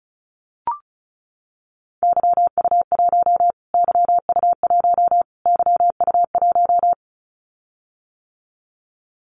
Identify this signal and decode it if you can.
Morse code, heard as a keyed tone, about 34 wpm, 700 Hz. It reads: YU1 YU1 YU1
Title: Callsign Prefixes for Top 25 DXCC - Repeated 3x - 35wpm